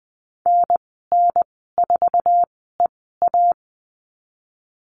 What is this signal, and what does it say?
Morse code, heard as a keyed tone, about 20 wpm, spaced as for 15 wpm, 700 Hz. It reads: NN4EA
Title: 082 Lesson 14g - Number 2 - Call Signs 20wpm